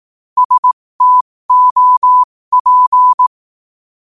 Morse code, heard as a keyed tone, about 18 wpm, spaced as for 15 wpm, 1000 Hz. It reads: STOP
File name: qotdfiles / 15wpm